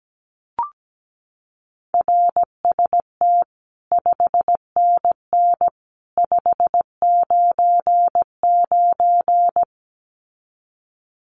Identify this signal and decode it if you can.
Morse code, heard as a keyed tone, about 17 wpm, 700 Hz. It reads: RST 5NN 599
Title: Common QSO Phrases 17wpm